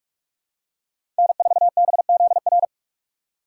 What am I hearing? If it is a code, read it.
Morse code, heard as a keyed tone, about 46 wpm, 700 Hz. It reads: N4BZR
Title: USA Call Signs - 5 Characters - 2x Character Spacing - 45wpm